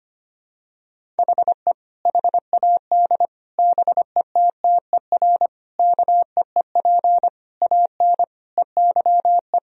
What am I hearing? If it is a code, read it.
Morse code, heard as a keyed tone, about 25 wpm, 700 Hz. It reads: HE HAD BETTER KEEP AN EYE
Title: Sentences from Top 1000 Words - Review Part 1 - 25wpm